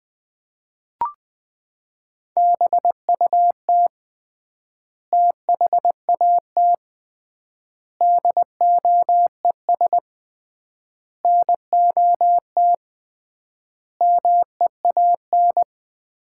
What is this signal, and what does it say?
Morse code, heard as a keyed tone, about 20 wpm, 700 Hz. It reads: BUT THAT DOES NOT MEAN
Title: 3x Word-Spacing - Sets of 5 Words - 20wpm